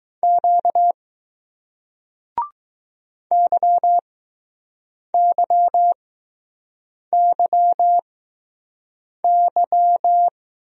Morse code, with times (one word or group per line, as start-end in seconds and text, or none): 0.2-0.9 Q
3.3-10.3 YYYY